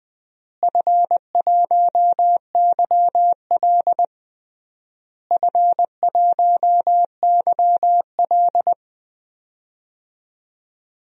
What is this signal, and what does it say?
Morse code, heard as a keyed tone, about 20 wpm, 700 Hz. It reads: F1YL F1YL
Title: DXCC-100 Call Signs - 4 Characters - Repeated 3x - 20wpm